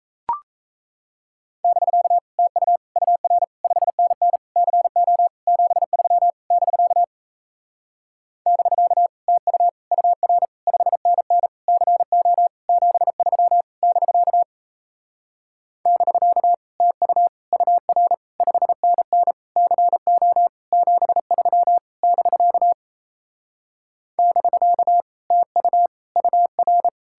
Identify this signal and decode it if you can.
Morse code, heard as a keyed tone, about 38 wpm, 700 Hz. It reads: <BK> TU UR 5NN CO 73 <BK> <BK> TU UR 5NN CO 73 <BK> <BK> TU UR 5NN CO 73 <BK> <BK> TU UR